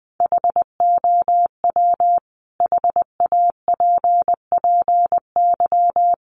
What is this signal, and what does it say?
Morse code, heard as a keyed tone, about 20 wpm, 700 Hz. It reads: HOW HAPPY
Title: Sentences from Top 1000 Words 20wpm